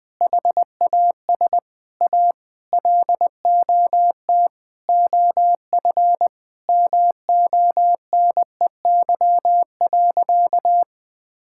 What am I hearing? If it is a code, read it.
Morse code, heard as a keyed tone, about 20 wpm, 700 Hz. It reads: HAS A LOT OF MONEY.
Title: Sentences from Top 1000 Words - Review Part 5 - 20wpm